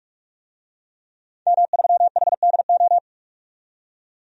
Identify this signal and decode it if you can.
Morse code, heard as a keyed tone, about 47 wpm, 700 Hz. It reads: M3HDO
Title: DXCC-100 Call Signs - 5 Characters - 45wpm